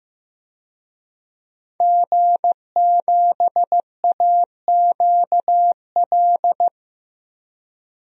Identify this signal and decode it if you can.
Morse code, heard as a keyed tone, about 15 wpm, 700 Hz. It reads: G7AQL
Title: DXCC-100 Call Signs - 5 Characters - 15wpm